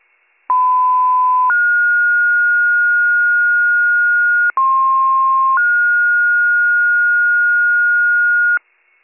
Two-tone page, heard as a one-second tone, then a three-second tone, about 1010 then 1430 Hz, about 1060 then 1430 Hz, with noise. It is heard from 0.5 to 8.6 s.